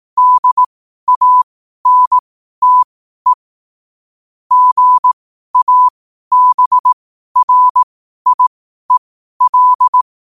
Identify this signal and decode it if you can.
Morse code, heard as a keyed tone, about 18 wpm, spaced as for 12 wpm, 1000 Hz. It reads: DANTE GABRIEL
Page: Quotes for Fri, 15 Aug 2025 in Morse Code at 12 words per minute.